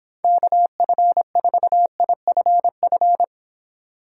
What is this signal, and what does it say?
Morse code, heard as a keyed tone, about 26 wpm, 700 Hz. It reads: KF4IFF
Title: Speed Racing - USA Call Signs 22wpm